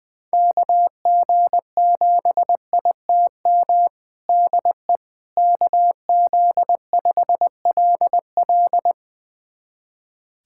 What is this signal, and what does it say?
Morse code, heard as a keyed tone, about 20 wpm, 700 Hz. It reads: KG7ITM DE KZ5LL
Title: USA Callsign DE Callsign 20wpm